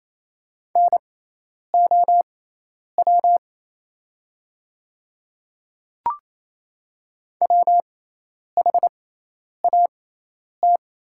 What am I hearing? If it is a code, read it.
Morse code, heard as a keyed tone, about 28 wpm, spaced as for 10 wpm, 700 Hz. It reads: NOW WHAT
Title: 6x Character Spacing - Top 300 Words - 28wpm